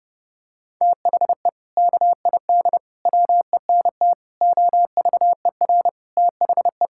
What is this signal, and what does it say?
Morse code, heard as a keyed tone, about 30 wpm, 700 Hz. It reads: THE KID WENT OVER THE
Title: Sentences from Top 1600 Words 30wpm